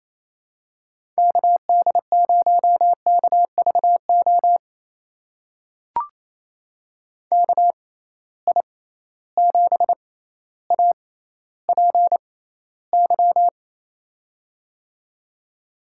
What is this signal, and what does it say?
Morse code, heard as a keyed tone, about 27 wpm, spaced as for 14 wpm, 700 Hz. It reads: KD0KVO KI7APY
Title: USA Call Signs - 6 Characters - 6x Character Spacing - 28wpm